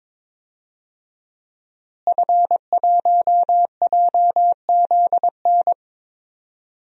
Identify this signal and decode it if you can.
Morse code, heard as a keyed tone, about 22 wpm, 700 Hz. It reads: F1JZN
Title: DXCC-100 Call Signs - 5 Characters - Repeated 2x - 22wpm